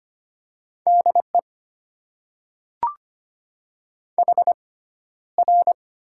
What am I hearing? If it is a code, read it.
Morse code, heard as a keyed tone, about 25 wpm, spaced as for 13 wpm, 690 Hz. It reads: DE HR
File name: Top 90 QSO Elements - 6x Character Spacing - 25wpm